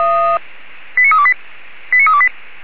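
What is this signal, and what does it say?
Five-tone selective call (ZVEI), heard as quick runs of five tones, tones of about 70 ms. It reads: 89328 89328
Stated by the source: ZVEI five-tone selective call 89327 received by SDR.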